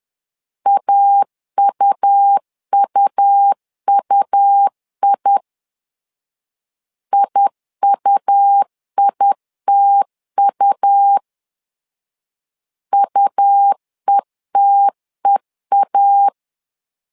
Morse code, heard as a keyed tone, about 10 wpm, 800 Hz. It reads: AUUUI IUITU UETEA
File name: LEZIONI / 2 UHDN / 2 U 10wpm 1622